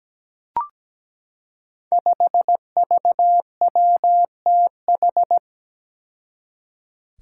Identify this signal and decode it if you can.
Morse code, heard as a keyed tone, about 17 wpm, 700 Hz. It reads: 5VWTH
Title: Five Challenging Letter-Number Combinations 17wpm